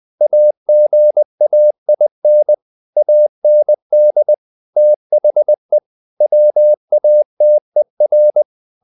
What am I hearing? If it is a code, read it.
Morse code, heard as a keyed tone, about 20 wpm, 600 Hz. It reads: AGAIN AND THE WATER
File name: the old man and the sea / the old man and the sea part 1 20wpm